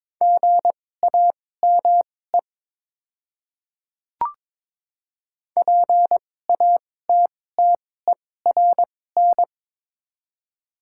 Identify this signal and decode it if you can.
Morse code, heard as a keyed tone, about 22 wpm, spaced as for 16 wpm, 700 Hz. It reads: GAME PATTERN
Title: 2x Character Spacing - Top 500 Words - 22wpm